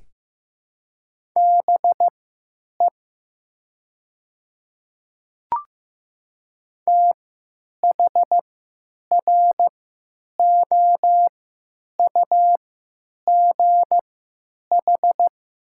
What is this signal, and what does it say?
Morse code, heard as a keyed tone, about 15 wpm, spaced as for 8 wpm, 700 Hz. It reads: BE THROUGH
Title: 3x Character Spacing - Top 200 Words - 15wpm